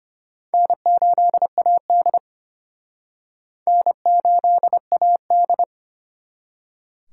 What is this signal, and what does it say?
Morse code, heard as a keyed tone, about 32 wpm, 700 Hz. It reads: N8AD N8AD